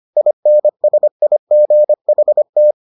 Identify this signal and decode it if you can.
Morse code, heard as a keyed tone, about 25 wpm, 600 Hz. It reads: INSIGHT